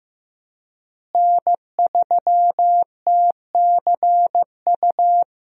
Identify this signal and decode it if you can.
Morse code, heard as a keyed tone, about 15 wpm, 700 Hz. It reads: N3TCU
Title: USA Call Signs - 5 Characters - Repeated 2x - 15wpm